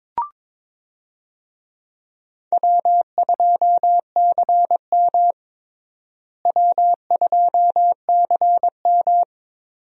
Morse code, heard as a keyed tone, about 22 wpm, 700 Hz. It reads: W2CM W2CM
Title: USA Call Signs - 4 Characters - Repeated 2x - 22wpm